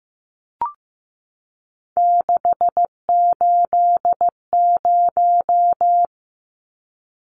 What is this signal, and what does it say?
Morse code, heard as a keyed tone, about 15 wpm, 700 Hz. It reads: 680
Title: Numbers - 3-Digit 15wpm